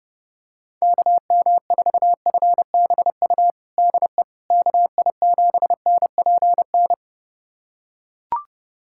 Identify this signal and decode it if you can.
Morse code, heard as a keyed tone, about 30 wpm, 700 Hz. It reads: KM4FBU DE KI7NPN
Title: USA Callsign DE Callsign 30wpm